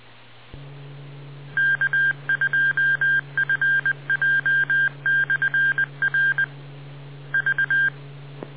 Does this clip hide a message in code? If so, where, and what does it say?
Yes: Morse code from 1.6 to 7.9 s, heard as a keyed tone, about 20 wpm, 1600 Hz. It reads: K2FJ/R V